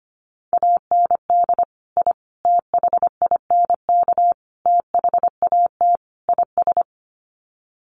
Morse code, heard as a keyed tone, about 25 wpm, 700 Hz. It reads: AND I THINK THAT IS
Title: Sets of 5 Words Encore2 25wpm